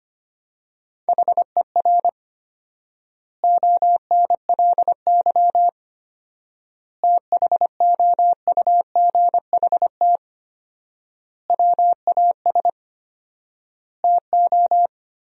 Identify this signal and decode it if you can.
Morse code, heard as a keyed tone, about 25 wpm, 700 Hz. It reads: HER ONLY THOUGHT WAS TO
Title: 4x Word-Spacing - Sentences from Top 400 Words - 25wpm